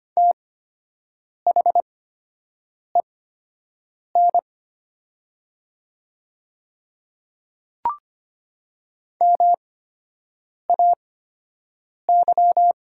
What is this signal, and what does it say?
Morse code, heard as a keyed tone, about 25 wpm, spaced as for 7 wpm, 700 Hz. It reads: THEN MAY
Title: 8x Character Spacing - Top 200 Words - 25wpm